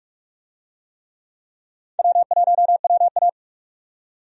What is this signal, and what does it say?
Morse code, heard as a keyed tone, about 46 wpm, 700 Hz. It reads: W1WA
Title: USA Call Signs - 4 Characters - 45wpm